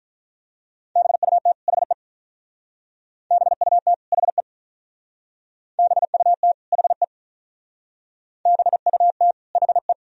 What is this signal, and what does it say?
Morse code, heard as a keyed tone, about 44 wpm, 700 Hz. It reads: BUT HE BUT HE BUT HE BUT HE